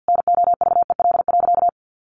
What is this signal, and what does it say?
Morse code, heard as a keyed tone, about 50 wpm, 700 Hz. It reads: NOVEL.
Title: Sentences from Top 1300 Words 50wpm